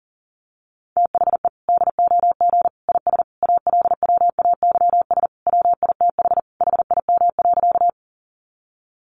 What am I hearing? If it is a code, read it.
Morse code, heard as a keyed tone, about 40 wpm, 700 Hz. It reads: THE DOG IS ALWAYS WITH HIM.
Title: Sentences from Top 400 Words 40wpm